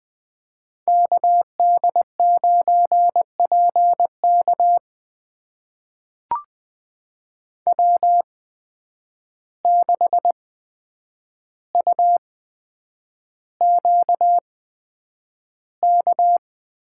Morse code, heard as a keyed tone, about 20 wpm, spaced as for 9 wpm, 700 Hz. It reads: KD9PK W6UQK
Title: USA Call Signs - 5 Characters - 8x Character Spacing - 20wpm